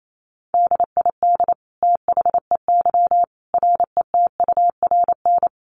Morse code, heard as a keyed tone, about 28 wpm, 700 Hz. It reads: DID THEY RETURN
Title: Sentences from Top 800 Words 28wpm